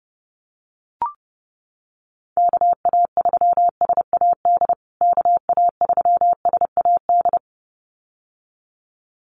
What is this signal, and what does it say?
Morse code, heard as a keyed tone, about 30 wpm, 700 Hz. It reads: KA3SAD KA3SAD
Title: USA Call Signs Repeated 2x 30wpm